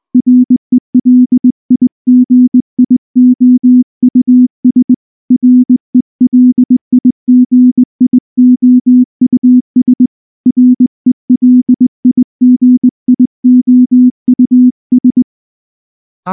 Morse code, heard as a keyed tone, about 19 wpm, 250 Hz. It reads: RELIGIOUS RELIGIOUS RELIGIOUS
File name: morse / morse-training-20WPM-4